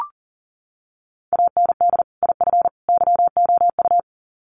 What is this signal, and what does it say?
Morse code, heard as a keyed tone, about 40 wpm, 700 Hz. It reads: AND IF YOU
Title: Sets of 3 Words 40wpm